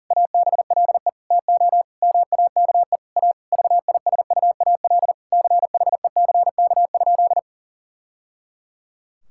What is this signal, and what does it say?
Morse code, heard as a keyed tone, about 40 wpm, 700 Hz. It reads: ABLE TO MAKE A VISUAL CHECK?
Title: Sentences from Top 1000 Words - Review Part 6 - 40wpm